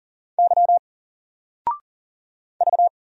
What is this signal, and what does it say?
Morse code, heard as a keyed tone, about 40 wpm, 700 Hz. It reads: Y V
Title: Single Letter-Number Alt 40wpm